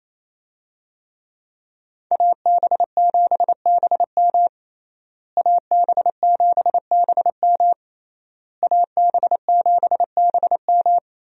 Morse code, heard as a keyed tone, about 28 wpm, 700 Hz. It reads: AB7BM AB7BM AB7BM
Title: USA Call Signs - 5 Characters - Repeated 3x - 28wpm